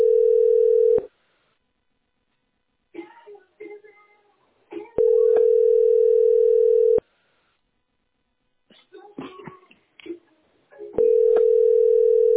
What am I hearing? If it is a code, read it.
Ringback tone (North America), 440 + 480 Hz, with speech.
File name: d247ad84-bb76-437c-99cf